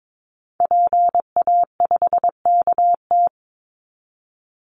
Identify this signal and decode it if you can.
Morse code, heard as a keyed tone, about 22 wpm, 700 Hz. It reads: PA5KT
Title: Call Signs Repeated 3x - 22wpm